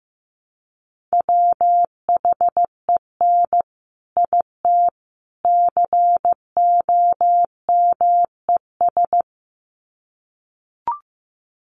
Morse code, heard as a keyed tone, about 15 wpm, 700 Hz. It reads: WHEN IT COMES
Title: Sets of 3 Words 15wpm